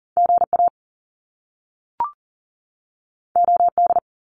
Morse code, heard as a keyed tone, about 41 wpm, 700 Hz. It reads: GA OD
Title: Top Two Letter Combinations in Words 40wpm